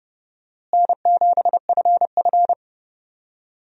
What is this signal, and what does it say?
Morse code, heard as a keyed tone, about 30 wpm, 700 Hz. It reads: N7FF
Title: USA Call Signs - 4 Characters - Repeated 2x - 30wpm